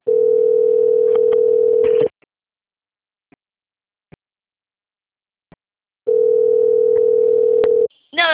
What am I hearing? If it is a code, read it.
Ringback tone (North America), 440 + 480 Hz.
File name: out- - - - - .68678